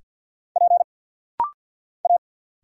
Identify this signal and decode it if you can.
Morse code, heard as a keyed tone, about 52 wpm, 700 Hz. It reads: P A
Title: Single Letters - Rapid-Fire 50wpm